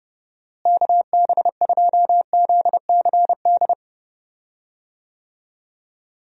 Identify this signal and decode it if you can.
Morse code, heard as a keyed tone, about 30 wpm, 700 Hz. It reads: KB2ZCD
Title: USA Call Signs - 6 Characters - Repeated 2x - 30wpm